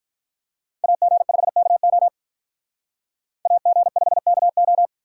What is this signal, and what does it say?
Morse code, heard as a keyed tone, about 45 wpm, 700 Hz. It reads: AG5KO AG5KO